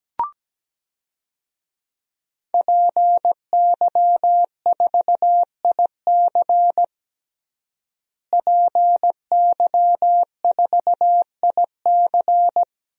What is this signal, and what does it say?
Morse code, heard as a keyed tone, about 17 wpm, 700 Hz. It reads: PY4IC PY4IC
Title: DXCC-100 Call Signs - 5 Characters - Repeated 2x - 17wpm